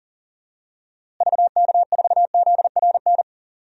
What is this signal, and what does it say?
Morse code, heard as a keyed tone, about 37 wpm, 700 Hz. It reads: VK4ZRN
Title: DXCC-100 Call Signs - 6 Characters - Repeated 2x - 40wpm